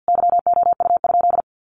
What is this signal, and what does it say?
Morse code, heard as a keyed tone, about 50 wpm, 700 Hz. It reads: YOU?
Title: Sentences from Top 400 Words 50wpm